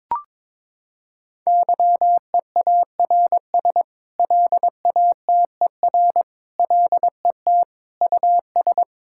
Morse code, heard as a keyed tone, about 21 wpm, 700 Hz. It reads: YEARS LATER LET US
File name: practice / 22wpm / Sentences with 8-Word Phrases - 22wpm